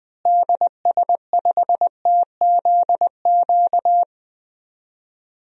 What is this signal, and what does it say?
Morse code, heard as a keyed tone, about 20 wpm, 700 Hz. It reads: DS5TZQ
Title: DXCC-100 Call Signs - 6 Characters - Repeated 3x - 20wpm